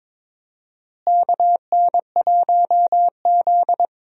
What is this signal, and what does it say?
Morse code, heard as a keyed tone, about 22 wpm, 700 Hz. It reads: KN1Z
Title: USA Call Signs - 4 Characters - Repeated 3x - 22wpm